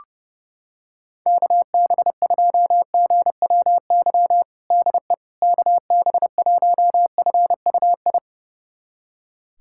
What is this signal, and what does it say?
Morse code, heard as a keyed tone, about 30 wpm, 700 Hz. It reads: KB2GWY DE KB1FUI
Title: USA Callsign DE Callsign - 6 Characters - 30wpm